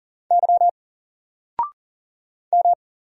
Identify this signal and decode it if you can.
Morse code, heard as a keyed tone, about 40 wpm, 700 Hz. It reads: Y M
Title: Single Character (Letter:Number:Punctuation:Prosign) 40wpm